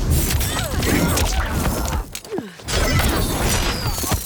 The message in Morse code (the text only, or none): none